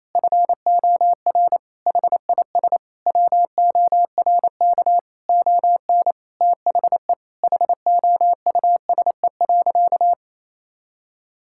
Morse code, heard as a keyed tone, about 28 wpm, 700 Hz. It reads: FOR HIS WORK ON THE HOUSE.
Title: Sentences from Top 1000 Words - Review Part 5 - 28wpm